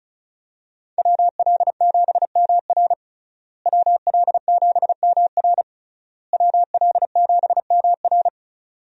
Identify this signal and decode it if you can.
Morse code, heard as a keyed tone, about 34 wpm, 700 Hz. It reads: WL7MR WL7MR WL7MR
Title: USA Call Signs - 5 Characters - Repeated 3x - 35wpm